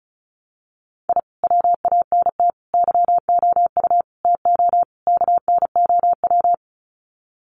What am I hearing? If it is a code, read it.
Morse code, heard as a keyed tone, about 34 wpm, 700 Hz. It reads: I WANT YOU TO KNOW
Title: Sets of 5 Words Encore 35wpm